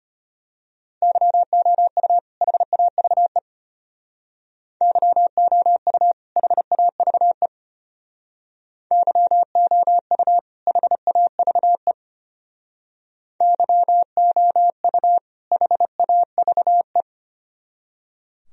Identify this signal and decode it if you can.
Morse code, heard as a keyed tone, about 32 wpm, 700 Hz. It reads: YOU HAVE YOU HAVE YOU HAVE YOU HAVE